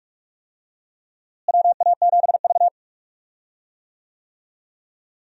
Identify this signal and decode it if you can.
Morse code, heard as a keyed tone, about 46 wpm, 700 Hz. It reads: WA7V
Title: USA Call Signs - 4 Characters - Repeated 2x - 45wpm